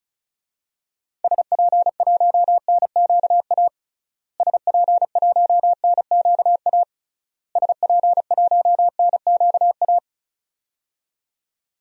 Morse code, heard as a keyed tone, about 36 wpm, 700 Hz. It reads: SP1NQA SP1NQA SP1NQA
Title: DXCC-100 Call Signs - 6 Characters - Repeated 3x - 35wpm